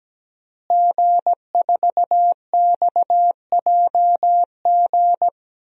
Morse code, heard as a keyed tone, about 17 wpm, 700 Hz. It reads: G4XJG
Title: DXCC-100 Call Signs - 5 Characters - Repeated 3x - 17wpm